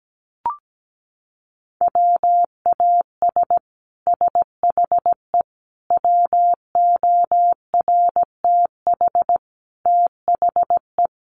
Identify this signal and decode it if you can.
Morse code, heard as a keyed tone, about 17 wpm, 700 Hz. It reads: WAS SHE WORTH THE
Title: Sentences from Top 1200 Words 17wpm